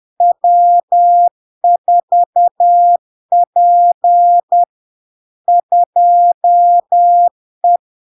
Morse code, heard as a keyed tone, about 10 wpm, 700 Hz. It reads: W4P 2E
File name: LARK-FD-2022-Log-in-CW-10wpm-700-Hz